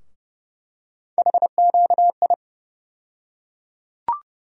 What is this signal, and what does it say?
Morse code, heard as a keyed tone, about 30 wpm, 700 Hz. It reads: HQI
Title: USA Call Sign Postfixes - 3 Letters 30wpm